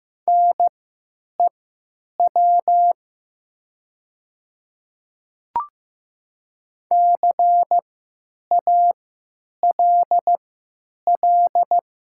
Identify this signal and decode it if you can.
Morse code, heard as a keyed tone, about 16 wpm, spaced as for 9 wpm, 700 Hz. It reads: NEW CALL
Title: 3x Character Spacing - Top 500 Words - 15wpm